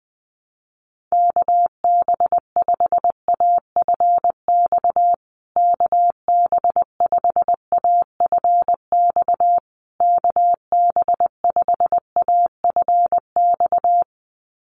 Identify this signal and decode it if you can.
Morse code, heard as a keyed tone, about 20 wpm, 700 Hz. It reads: KB5AFX KB5AFX KB5AFX
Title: USA Call Signs Repeated 3x 20wpm